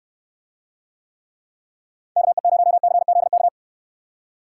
Morse code, heard as a keyed tone, about 69 wpm, 700 Hz. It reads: KE0CZK